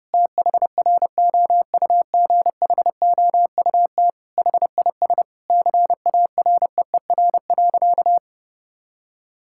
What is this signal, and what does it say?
Morse code, heard as a keyed tone, about 32 wpm, 700 Hz. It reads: THROUGHOUT HIS CAREER.
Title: Sentences from Top 1000 Words - Review Part 6 - 30wpm